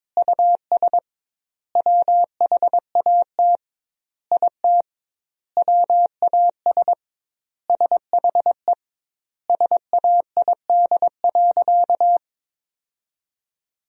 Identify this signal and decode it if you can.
Morse code, heard as a keyed tone, about 22 wpm, 700 Hz. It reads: US WHAT IT WAS SHE SAID.
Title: 2x Word-Spacing - Sentences from Top 400 Words - 22wpm